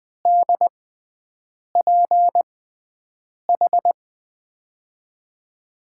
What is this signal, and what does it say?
Morse code, heard as a keyed tone, about 20 wpm, spaced as for 7 wpm, 700 Hz. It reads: DPH
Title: USA Call Signs - 6 Characters - 6x Character Spacing - 20wpm